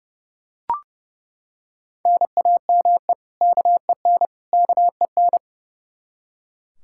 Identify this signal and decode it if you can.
Morse code, heard as a keyed tone, about 32 wpm, 700 Hz. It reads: NAME KEN KEN
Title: Practice QSO - Exchange 1 - Beginning - 30wpm